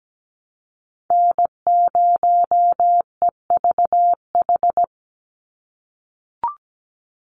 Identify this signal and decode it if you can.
Morse code, heard as a keyed tone, about 17 wpm, 700 Hz. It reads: N0EVH
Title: NA SOTA Chaser Callsigns 17wpm